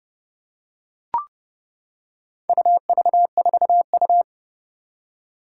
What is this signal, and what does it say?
Morse code, heard as a keyed tone, about 30 wpm, 690 Hz. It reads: UV4U
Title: Four Challenging Letter-Number Combinations 30wpm